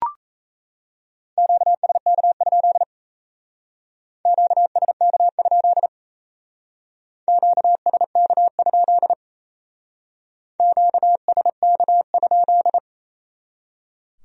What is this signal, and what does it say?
Morse code, heard as a keyed tone, about 32 wpm, 700 Hz. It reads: QSK? QSK? QSK? QSK?
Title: Speed-Racing Q-Codes - 28wpm